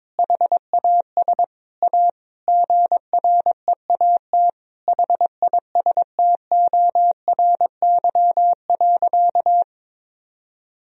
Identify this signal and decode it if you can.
Morse code, heard as a keyed tone, about 22 wpm, 700 Hz. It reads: HAS A GREAT HISTORY.
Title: Sentences from Top 1000 Words - Review Part 4 - 22wpm